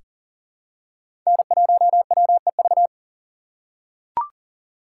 Morse code, heard as a keyed tone, about 40 wpm, 700 Hz. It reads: N1WEV
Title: USA Call Signs - 2x Character Spacing - 40wpm